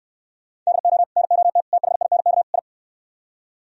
Morse code, heard as a keyed tone, about 64 wpm, 700 Hz. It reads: DO NOT THINK I